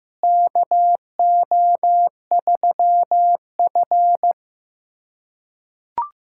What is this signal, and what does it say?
Morse code, heard as a keyed tone, about 15 wpm, 700 Hz. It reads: KO3F
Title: USA Callsign DE Callsign - 4 Characters - 15wpm